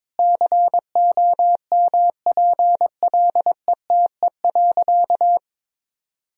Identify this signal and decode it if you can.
Morse code, heard as a keyed tone, about 22 wpm, 700 Hz. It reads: COMPLETE.
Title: Sentences from Top 400 Words and Abbreviations 22wpm